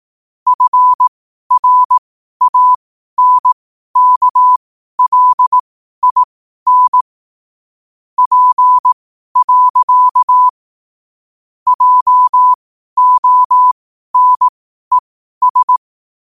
Morse code, heard as a keyed tone, about 18 wpm, spaced as for 12 wpm, 1000 Hz. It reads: FRANKLIN P. JONES
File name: qotdfiles / 12wpm